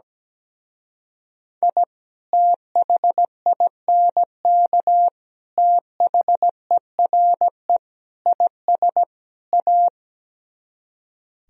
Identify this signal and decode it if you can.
Morse code, heard as a keyed tone, about 17 wpm, 700 Hz. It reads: I THINK THERE IS A
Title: Speed-Racing - Sets of 5 Words 17wpm